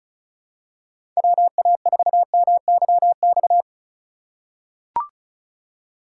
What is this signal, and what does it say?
Morse code, heard as a keyed tone, about 36 wpm, 700 Hz. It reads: WA4MYX
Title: USA Call Signs - 6 Characters - 4x Character Spacing - 35wpm